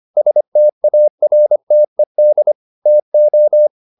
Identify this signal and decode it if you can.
Morse code, heard as a keyed tone, about 25 wpm, 600 Hz. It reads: STARTED TO
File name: the old man and the sea part 1 25wpm